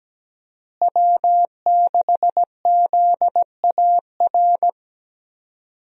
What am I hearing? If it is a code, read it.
Morse code, heard as a keyed tone, about 17 wpm, 700 Hz. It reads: W6ZAR
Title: USA Call Signs - 5 Characters - Repeated 2x - 17wpm